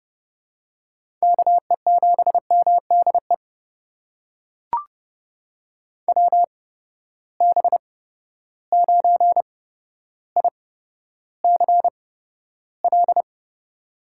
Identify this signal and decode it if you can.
Morse code, heard as a keyed tone, about 30 wpm, spaced as for 13 wpm, 700 Hz. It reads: KE7MDE WB9ICL
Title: USA Call Signs - 6 Characters - 8x Character Spacing - 30wpm